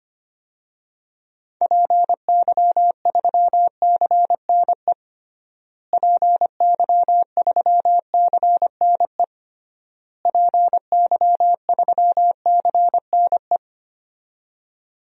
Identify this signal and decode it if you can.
Morse code, heard as a keyed tone, about 25 wpm, 700 Hz. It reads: PY3CNE PY3CNE PY3CNE
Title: DXCC-100 Call Signs - 6 Characters - Repeated 3x - 25wpm